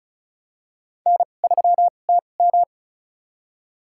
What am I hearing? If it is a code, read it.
Morse code, heard as a keyed tone, about 36 wpm, spaced as for 25 wpm, 700 Hz. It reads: N3TM
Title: USA Call Signs - 4 Characters - 2x Character Spacing - 35wpm